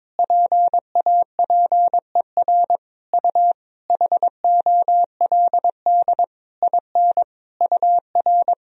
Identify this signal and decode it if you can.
Morse code, heard as a keyed tone, about 22 wpm, 700 Hz. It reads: PAPER U HOLD IN UR
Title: Sentences from Top 400 Words and Abbreviations 22wpm